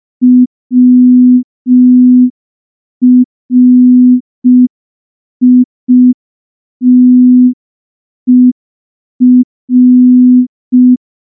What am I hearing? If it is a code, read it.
Morse code, heard as a keyed tone, about 5 wpm, 250 Hz. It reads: WRITER
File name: morse / morse-training-05WPM